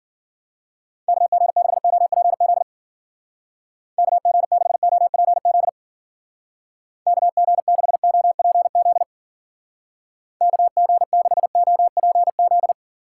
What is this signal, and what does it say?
Morse code, heard as a keyed tone, about 53 wpm, 700 Hz. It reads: KG6OPZ KG6OPZ KG6OPZ KG6OPZ